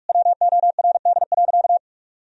Morse code, heard as a keyed tone, about 46 wpm, 700 Hz. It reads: WORD.
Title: Sentences from Top 1000 Words - Review Part 2 - 45wpm